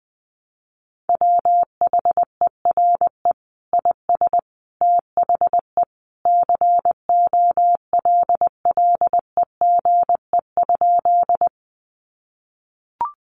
Morse code, heard as a keyed tone, about 20 wpm, 700 Hz. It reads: WHERE IS THE COLLEGE?
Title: Sentences from Top 1300 Words 20wpm